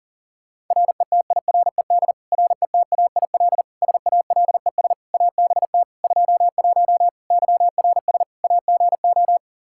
Morse code, heard as a keyed tone, about 40 wpm, 690 Hz. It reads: RETIRED RETAIL SALES ABT 21 YRS AGO
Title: Practice QSO - Exchange 3 - More Info - 40wpm